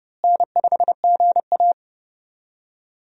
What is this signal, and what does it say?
Morse code, heard as a keyed tone, about 30 wpm, 700 Hz. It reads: N5GA
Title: USA Call Signs - 4 Characters - Repeated 3x - 30wpm